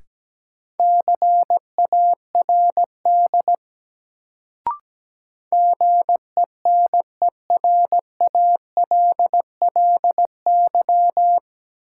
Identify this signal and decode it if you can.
Morse code, heard as a keyed tone, about 17 wpm, 700 Hz. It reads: CARD GENERALLY
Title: Sentences from Top 600 Words - New Word Drill - 17wpm